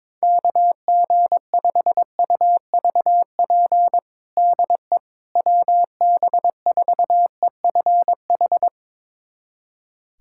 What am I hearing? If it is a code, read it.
Morse code, heard as a keyed tone, about 22 wpm, 700 Hz. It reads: KG5UVP DE WB4EFH
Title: USA Callsign DE Callsign 22wpm